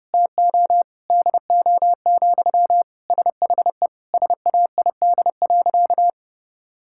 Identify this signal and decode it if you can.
Morse code, heard as a keyed tone, about 32 wpm, 700 Hz. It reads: TO DO, SHE SAID.
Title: Sentences from Top 1000 Words - Review Part 5 - 30wpm